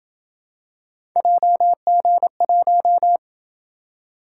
Morse code, heard as a keyed tone, about 26 wpm, 700 Hz. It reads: JG1